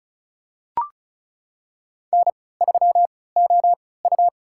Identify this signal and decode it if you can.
Morse code, heard as a keyed tone, about 34 wpm, spaced as for 20 wpm, 700 Hz. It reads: N3OU
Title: USA Call Signs - 3x Character Spacing - 35wpm